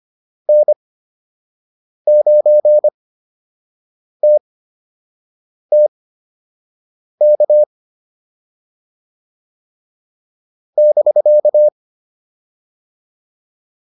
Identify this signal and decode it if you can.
Morse code, heard as a keyed tone, about 25 wpm, spaced as for 6 wpm, 600 Hz. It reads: N9TTK <BK>